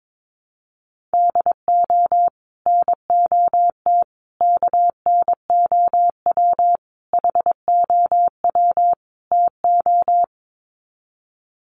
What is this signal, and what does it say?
Morse code, heard as a keyed tone, about 22 wpm, 700 Hz. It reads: DO NOT KNOW HOW TO
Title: Sets of 5 Words 22wpm